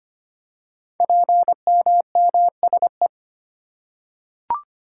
Morse code, heard as a keyed tone, about 25 wpm, 700 Hz. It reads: PMMSE
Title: Code Groups - 5 Letters - 25wpm